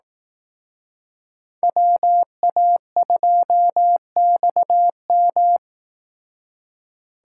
Morse code, heard as a keyed tone, about 19 wpm, 700 Hz. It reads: WA2XM
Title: Speed Racing - USA 2x2 Call Signs 15wpm